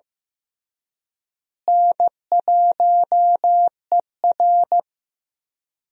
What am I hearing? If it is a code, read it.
Morse code, heard as a keyed tone, about 15 wpm, 700 Hz. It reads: N1ER
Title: USA Call Signs - 4 Characters - Repeated 3x - 15wpm